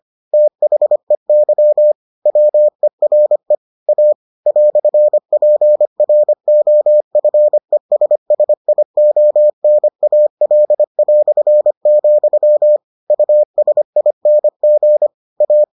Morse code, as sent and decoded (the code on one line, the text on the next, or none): - .... -.-- / .-- . .-. / .- / .-..-. .--. .-. --- ..-. ... ... .. --- -. .- .-.. .-..-. --..-- / ..- ... .. -. --. / .-
THEY WERE A "PROFESSIONAL", USING A